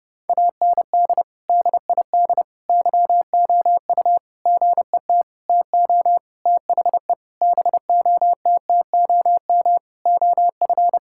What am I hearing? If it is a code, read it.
Morse code, heard as a keyed tone, about 32 wpm, 700 Hz. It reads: AND DID YOU GET TO THE BOTTOM OF
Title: Sentences from Top 1000 Words - Review Part 6 - 30wpm